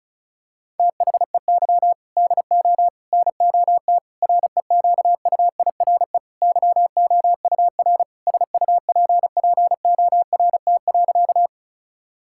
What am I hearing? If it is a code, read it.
Morse code, heard as a keyed tone, about 36 wpm, 700 Hz. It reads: THEY DO NOT REQUIRE YOUR SUPPORT.
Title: Sentences from Top 1000 Words - Review Part 6 - 35wpm